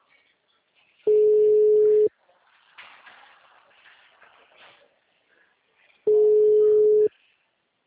Ringback tone (Europe, 425 Hz).